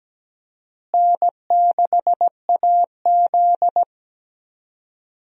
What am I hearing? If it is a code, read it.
Morse code, heard as a keyed tone, about 17 wpm, 700 Hz. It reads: N6AZ
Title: USA Call Signs - 4 Characters - Repeated 2x - 17wpm